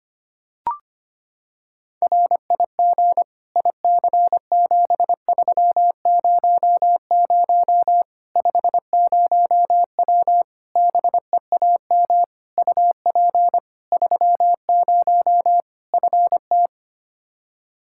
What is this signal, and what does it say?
Morse code, heard as a keyed tone, about 25 wpm, 700 Hz. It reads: RIG IC7300 50W BEAM UP 30 FT
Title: Practice QSO - Exchange 2 - Info - 25wpm